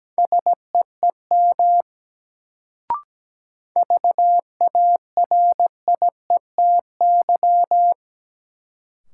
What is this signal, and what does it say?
Morse code, heard as a keyed tone, about 17 wpm, 700 Hz. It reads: SEEM VARIETY
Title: Sets of 3 Words - New Word Drill - 17wpm